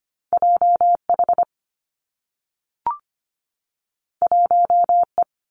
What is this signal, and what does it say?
Morse code, heard as a keyed tone, about 25 wpm, 700 Hz. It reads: JH 1E
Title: Two Challenging Letter-Number Combinations 25wpm